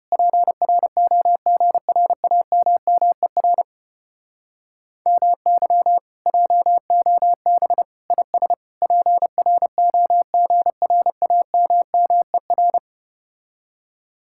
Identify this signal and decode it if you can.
Morse code, heard as a keyed tone, about 33 wpm, 700 Hz. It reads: PROGRAMMER MY JOB IS PROGRAMMER